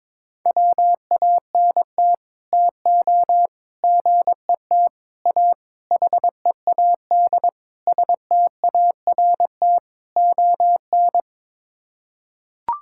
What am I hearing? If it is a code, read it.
Morse code, heard as a keyed tone, about 22 wpm, 700 Hz. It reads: WANT TO GET A HEAD START ON
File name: practice / 22wpm / Sets of 7 Words - 22wpm